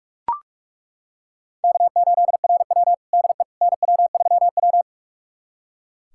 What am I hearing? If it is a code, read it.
Morse code, heard as a keyed tone, about 42 wpm, 700 Hz. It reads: K8RW DE NW3W
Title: USA Callsign DE Callsign - 4 Characters - 45wpm